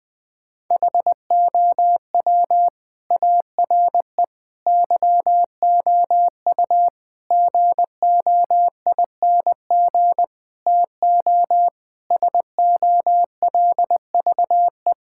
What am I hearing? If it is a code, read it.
Morse code, heard as a keyed tone, about 20 wpm, 700 Hz. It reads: HOW ARE YOU GOING TO SOLVE
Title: Sentences with 5-Word Phrases - Spoken Beginning - 20wpm